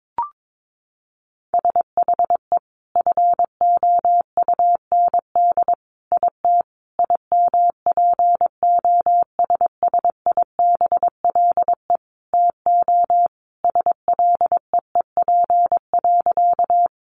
Morse code, heard as a keyed tone, about 22 wpm, 700 Hz. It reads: SHE FOUND IT IMPOSSIBLE TO SLEEP.
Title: Sentences from Top 1100 Words 22wpm